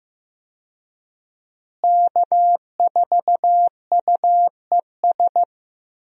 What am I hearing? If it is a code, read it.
Morse code, heard as a keyed tone, about 15 wpm, 700 Hz. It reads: K4UES
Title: USA Call Signs - 5 Characters - Repeated 3x - 15wpm